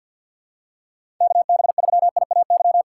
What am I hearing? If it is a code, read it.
Morse code, heard as a keyed tone, about 50 wpm, 700 Hz. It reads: KB3IAY
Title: USA Call Signs - 6 Characters - 3x Character Spacing - 50wpm